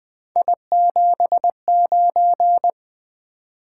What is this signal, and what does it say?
Morse code, heard as a keyed tone, about 20 wpm, 700 Hz. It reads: I79
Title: Three Letter-Number Combinations - 20wpm